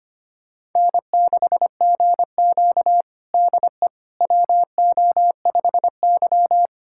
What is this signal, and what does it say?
Morse code, heard as a keyed tone, about 25 wpm, 700 Hz. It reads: N6GQ DE WO5Y
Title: USA Callsign DE Callsign - 4 Characters - 25wpm